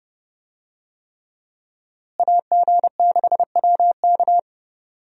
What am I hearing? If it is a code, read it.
Morse code, heard as a keyed tone, about 32 wpm, 700 Hz. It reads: AG6WK
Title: USA Call Signs - 5 Characters - Repeated 3x - 30wpm